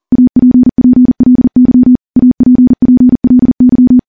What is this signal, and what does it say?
Morse code, heard as a keyed tone, about 41 wpm, 250 Hz. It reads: APPLY APPLY
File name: morse / morse-training-40WPM